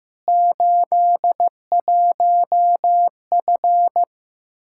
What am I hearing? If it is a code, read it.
Morse code, heard as a keyed tone, about 16 wpm, 700 Hz. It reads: 81F
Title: Practice QSO - Exchange 2 - Info - 15wpm